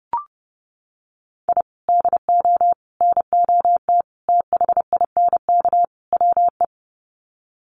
Morse code, heard as a keyed tone, about 30 wpm, 700 Hz. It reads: I DO NOT THINK WE
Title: Sets of 5 Words 30wpm